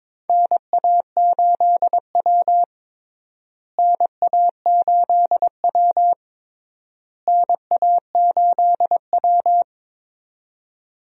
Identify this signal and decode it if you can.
Morse code, heard as a keyed tone, about 21 wpm, 700 Hz. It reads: NA8W NA8W NA8W
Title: USA Call Signs - 4 Characters - Repeated 3x - 22wpm